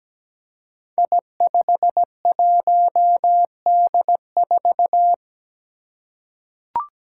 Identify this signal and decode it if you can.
Morse code, heard as a keyed tone, about 17 wpm, 700 Hz. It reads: I51D4
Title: Code Groups - 5 Letter:Numbers - 17wpm